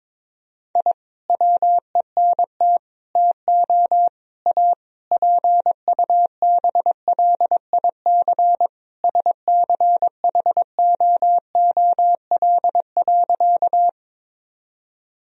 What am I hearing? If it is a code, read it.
Morse code, heard as a keyed tone, about 22 wpm, 690 Hz. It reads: I WENT TO A PUBLIC SCHOOL.
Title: Sentences from Top 1000 Words - Review Part 3 - 22wpm